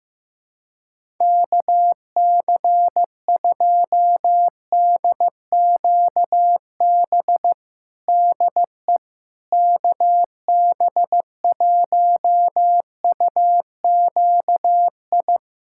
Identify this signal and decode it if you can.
Morse code, heard as a keyed tone, about 15 wpm, 700 Hz. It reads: KC2DQB DE KB1UQI
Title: USA Callsign DE Callsign 15wpm